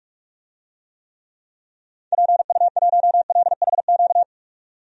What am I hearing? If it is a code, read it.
Morse code, heard as a keyed tone, about 46 wpm, 700 Hz. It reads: PU1LHQ
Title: DXCC-100 Call Signs - 6 Characters - 45wpm